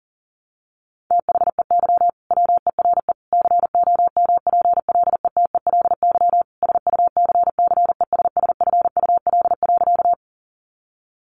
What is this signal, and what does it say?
Morse code, heard as a keyed tone, about 40 wpm, 700 Hz. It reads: THEY WERE COMPLETELY SUCCESSFUL.
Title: Sentences from Top 1100 Words 40wpm